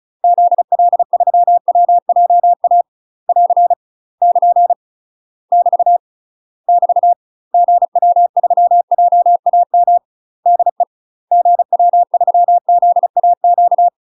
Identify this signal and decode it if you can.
Morse code, heard as a keyed tone, about 35 wpm, 700 Hz. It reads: ZL3WJA <AR> <KN> <BT> <BT> GW3JAM DE GW3ZAQ
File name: UK MASTER QSOs TRAINING / 35WPM / UK MASTER QSOs 0016